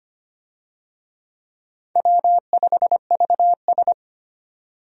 Morse code, heard as a keyed tone, about 25 wpm, 700 Hz. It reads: W5VS
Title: USA Call Signs - 4 Characters - 25wpm